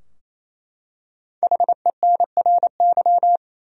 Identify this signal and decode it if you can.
Morse code, heard as a keyed tone, about 28 wpm, 700 Hz. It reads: HENRY
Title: USA Common Names - Max 5 Letters - 28wpm